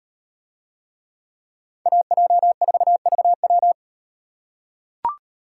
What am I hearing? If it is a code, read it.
Morse code, heard as a keyed tone, about 37 wpm, 700 Hz. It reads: AJ4VW